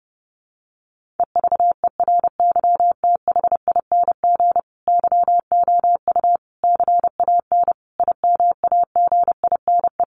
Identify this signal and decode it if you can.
Morse code, heard as a keyed tone, about 30 wpm, 700 Hz. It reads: EVERYTHING YOU CAN IMAGINE
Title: Sentences from Top 1400 Words 30wpm